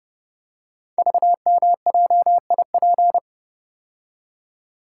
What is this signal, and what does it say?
Morse code, heard as a keyed tone, about 30 wpm, 700 Hz. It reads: VMJIP
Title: Code Groups - 5 Letter:Numbers - 30wpm